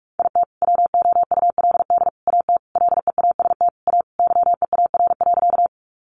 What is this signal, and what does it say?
Morse code, heard as a keyed tone, about 44 wpm, 700 Hz. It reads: IT WOULD AT LEAST A YEAR.
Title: Sentences from Top 700 Words 45wpm